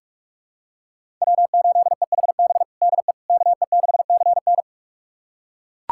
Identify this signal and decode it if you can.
Morse code, heard as a keyed tone, about 42 wpm, 700 Hz. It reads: W8EHB DE KE6CN
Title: USA Callsign DE Callsign - 5 Characters - 45wpm